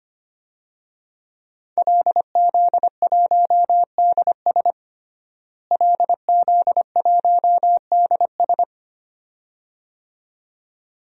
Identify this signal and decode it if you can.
Morse code, heard as a keyed tone, about 25 wpm, 700 Hz. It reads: LZ1DS LZ1DS
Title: DXCC-100 Call Signs - 5 Characters - Repeated 2x - 25wpm